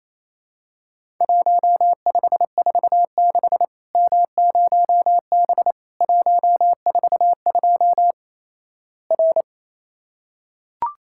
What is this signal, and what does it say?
Morse code, heard as a keyed tone, about 28 wpm, 700 Hz. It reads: 1546 M0B 142 R
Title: Worked All Europe DX Contest - QTC Exchange - 28wpm